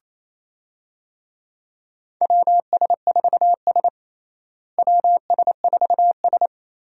Morse code, heard as a keyed tone, about 28 wpm, 700 Hz. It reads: WS4S WS4S
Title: USA Call Signs - 4 Characters - Repeated 2x - 28wpm